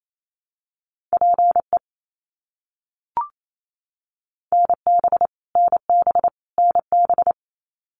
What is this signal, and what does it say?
Morse code, heard as a keyed tone, about 28 wpm, 700 Hz. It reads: PE NB NB NB
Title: Canadian Provinces and Territories - Repeated 3x - 28wpm